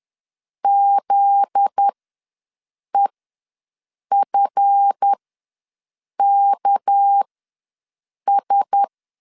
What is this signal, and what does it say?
Morse code, heard as a keyed tone, about 10 wpm, spaced as for 6 wpm, 800 Hz. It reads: ZEFKS
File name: LEZIONI / 9 28 ws 10wpm 4124